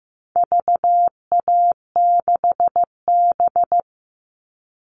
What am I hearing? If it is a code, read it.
Morse code, heard as a keyed tone, about 15 wpm, 700 Hz. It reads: VA6B
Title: All Cut Numbers - 4-Digit 15wpm